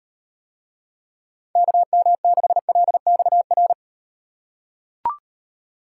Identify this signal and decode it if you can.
Morse code, heard as a keyed tone, about 38 wpm, 700 Hz. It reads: KM6LXR